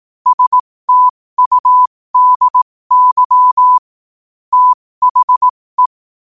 Morse code, heard as a keyed tone, about 18 wpm, spaced as for 15 wpm, 1000 Hz. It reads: STUDY THE
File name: qotdfiles / 15wpm